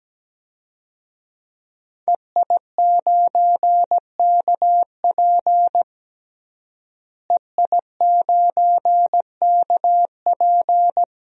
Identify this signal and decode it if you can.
Morse code, heard as a keyed tone, about 17 wpm, 700 Hz. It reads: EI9KP EI9KP
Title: DXCC-100 Call Signs - 5 Characters - Repeated 2x - 17wpm